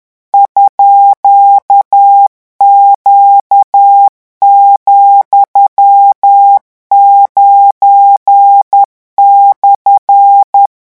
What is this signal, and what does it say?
Morse code, heard as a keyed tone, about 10 wpm, 800 Hz. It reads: _Q,9/
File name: morse mp3 / revision13-vitesse 10 mots